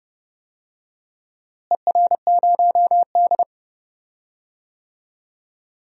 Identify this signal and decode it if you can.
Morse code, heard as a keyed tone, about 30 wpm, 700 Hz. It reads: ER0D
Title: DXCC-100 Call Signs - 4 Characters - Repeated 3x - 30wpm